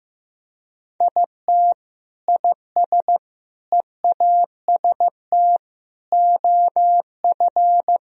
Morse code, heard as a keyed tone, about 15 wpm, 700 Hz. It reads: IT IS EAST OF
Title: Sentences from Top 1000 Words - Review Part 3 - 15wpm